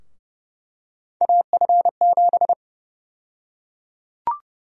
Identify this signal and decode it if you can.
Morse code, heard as a keyed tone, about 30 wpm, 700 Hz. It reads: AF7
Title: USA Call Sign Prefixes - 30wpm